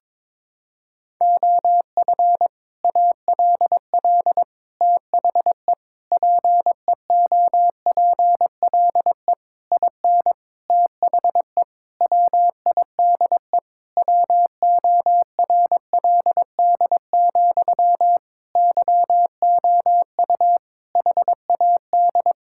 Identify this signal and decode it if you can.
Morse code, heard as a keyed tone, about 22 wpm, 700 Hz. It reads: OF ALL THE PEOPLE IN THE WIDE WORLD, YOU HAD
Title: Sentences from Top 1000 Words - Review Part 6 - 22wpm